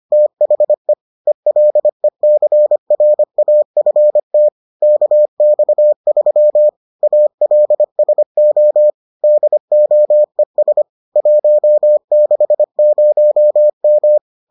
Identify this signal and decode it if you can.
Morse code, heard as a keyed tone, about 25 wpm, 600 Hz. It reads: THE ELECRAFT KX3 ALSO DOES 160M